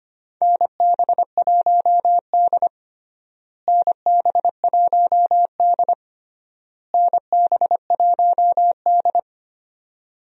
Morse code, heard as a keyed tone, about 25 wpm, 700 Hz. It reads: NB1D NB1D NB1D
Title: USA Call Signs - 4 Characters - Repeated 3x - 25wpm